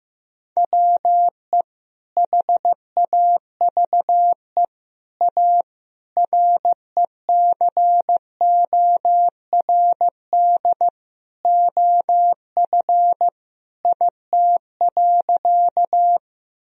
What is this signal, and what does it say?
Morse code, heard as a keyed tone, about 15 wpm, 700 Hz. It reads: WE HAVE A RECORD OF IT.
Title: Sentences from Top 1000 Words - Review Part 3 - 15wpm